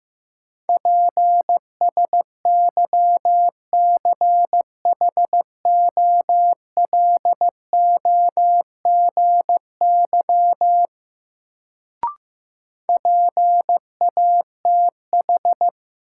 Morse code, heard as a keyed tone, about 15 wpm, 700 Hz. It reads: PSYCHOLOGY PATH
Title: Sentences from Top 1400 Words - New Word Drill - 15wpm